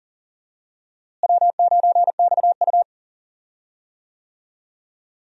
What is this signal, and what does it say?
Morse code, heard as a keyed tone, about 40 wpm, 700 Hz. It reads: W9XU
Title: USA Call Signs - 4 Characters - Repeated 2x - 40wpm